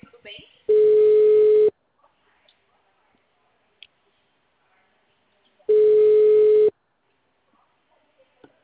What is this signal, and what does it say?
Ringback tone (Europe, 425 Hz), with speech.